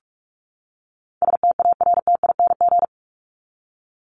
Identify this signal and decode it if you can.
Morse code, heard as a keyed tone, about 44 wpm, 700 Hz. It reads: STARTING
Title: ING Suffix - Top 100 Words with Base Words 45wpm